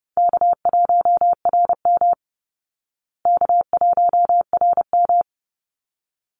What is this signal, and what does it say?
Morse code, heard as a keyed tone, about 30 wpm, 700 Hz. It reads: K1RM K1RM
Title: Easy Call Signs Repeated 2x - 30wpm